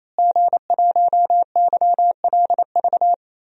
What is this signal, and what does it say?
Morse code, heard as a keyed tone, about 27 wpm, 700 Hz. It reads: G1YLV
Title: DXCC-100 Call Signs - 5 Characters - Repeated 3x - 28wpm